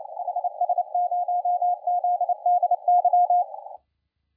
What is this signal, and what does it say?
Morse code, heard as a keyed tone, about 28 wpm, 710 Hz, with radio noise. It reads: HS0ZDY
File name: la4c / sac-cw-2017 / CONTEST AUDIO 1100Z